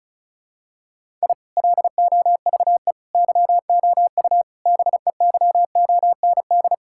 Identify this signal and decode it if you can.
Morse code, heard as a keyed tone, about 36 wpm, 700 Hz. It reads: I LOVE YOU BEYOND
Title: Sentences from Top 1000 Words - Review Part 6 - 35wpm